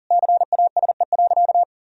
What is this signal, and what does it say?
Morse code, heard as a keyed tone, about 40 wpm, 700 Hz. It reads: CASE.
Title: Sentences from Top 1000 Words - Review Part 4 - 40wpm